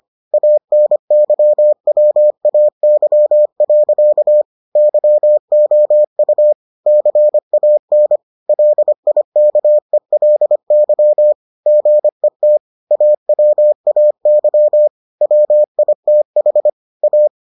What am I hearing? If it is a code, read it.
Morse code, heard as a keyed tone, about 25 wpm, 600 Hz. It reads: ANYWAY. YOU CAN LIKELY GET AWAY WITH A